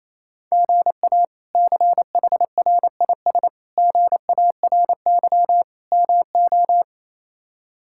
Morse code, heard as a keyed tone, about 28 wpm, 700 Hz. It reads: GA CHRIS GARY MO
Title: K1USN SST QSO Practice - 28wpm